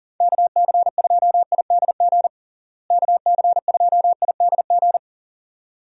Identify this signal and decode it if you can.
Morse code, heard as a keyed tone, about 40 wpm, 700 Hz. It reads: KC2IDG KC2IDG
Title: USA Call Signs - 6 Characters - Repeated 2x - 40wpm